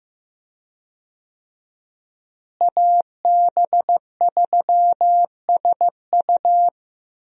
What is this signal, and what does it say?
Morse code, heard as a keyed tone, about 15 wpm, 700 Hz. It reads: AB3SU
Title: USA Call Signs - 5 Characters - Repeated 3x - 15wpm